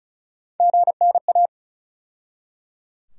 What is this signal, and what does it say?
Morse code, heard as a keyed tone, about 34 wpm, 700 Hz. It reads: GNA
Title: USA Call Sign Postfixes - 3 Letters 35wpm